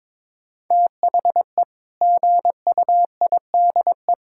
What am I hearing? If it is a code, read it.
Morse code, heard as a keyed tone, about 21 wpm, 700 Hz. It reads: THE GUIDE
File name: practice / 22wpm / Sentences with 8-Word Phrases - 22wpm